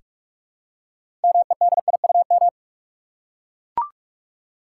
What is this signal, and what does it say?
Morse code, heard as a keyed tone, about 42 wpm, 700 Hz. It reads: MEDIUM
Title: Six Letter Words - 45wpm